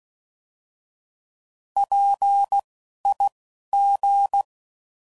Morse code, heard as a keyed tone, about 16 wpm, spaced as for 12 wpm, 800 Hz. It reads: PIG
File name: Prefixes / By First Letter / Ps